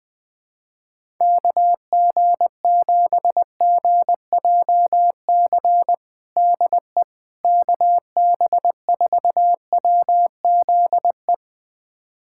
Morse code, heard as a keyed tone, about 20 wpm, 700 Hz. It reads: KG7GJC DE KB4WZE
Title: USA Callsign DE Callsign 20wpm